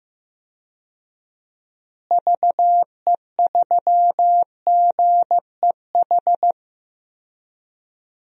Morse code, heard as a keyed tone, about 15 wpm, 700 Hz. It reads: VE3GEH
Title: DXCC-100 Call Signs - 6 Characters - Repeated 3x - 15wpm